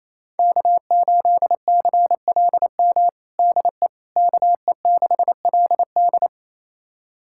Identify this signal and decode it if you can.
Morse code, heard as a keyed tone, about 28 wpm, 700 Hz. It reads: K8CLM DE KE6LD
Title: USA Callsign DE Callsign - 5 Characters - 28wpm